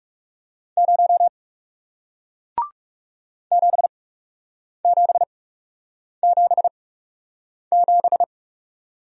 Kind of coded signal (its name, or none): Morse code